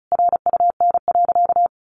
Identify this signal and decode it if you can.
Morse code, heard as a keyed tone, about 34 wpm, 700 Hz. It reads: RUN.
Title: Sentences from Top 1500 Words 35wpm